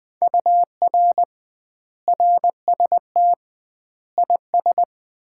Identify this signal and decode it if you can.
Morse code, heard as a keyed tone, about 20 wpm, 700 Hz. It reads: UR RST IS
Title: Common QSO Phrases - 2x Word-Spacing - 20wpm